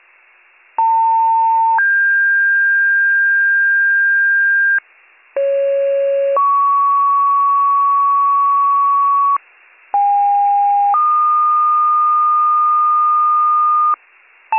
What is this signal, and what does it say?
Two-tone page, heard as a one-second tone, then a three-second tone, about 900 then 1600 Hz, about 550 then 1120 Hz, about 810 then 1220 Hz. It is heard from 0.8 to 14.0 s.